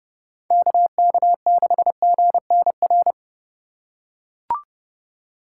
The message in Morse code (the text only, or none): KK6GNR